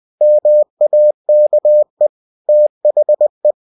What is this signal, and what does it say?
Morse code, heard as a keyed tone, about 20 wpm, 600 Hz. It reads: MAKE THE
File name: cw-books / winnie-the-pooh / winnie-20wpm-page-0009